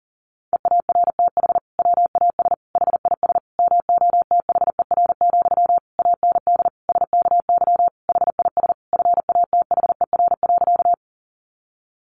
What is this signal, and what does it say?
Morse code, heard as a keyed tone, about 40 wpm, 700 Hz. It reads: EARTH WAS HIS MOTHER, AND SKY HIS FATHER.
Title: Sentences from Top 1200 Words 40wpm